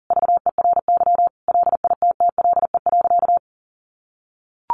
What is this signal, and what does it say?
Morse code, heard as a keyed tone, about 40 wpm, 690 Hz. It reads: VERY LITTLE.
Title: Sentences from Top 800 Words 40wpm